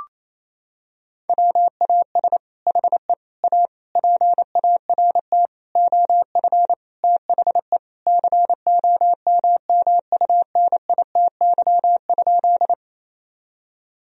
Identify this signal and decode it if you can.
Morse code, heard as a keyed tone, about 28 wpm, 700 Hz. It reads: WAS HE A PART OF THE COMMUNITY?
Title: Sentences from Top 1000 Words - Review Part 4 - 28wpm